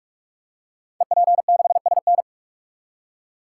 Morse code, heard as a keyed tone, about 44 wpm, 690 Hz. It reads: EP6SN
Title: Code Groups - 5 Letter:Numbers - 45wpm